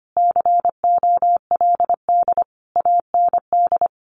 Morse code, heard as a keyed tone, about 25 wpm, 700 Hz. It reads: COLD AND
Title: Sentences from Top 1400 Words 25wpm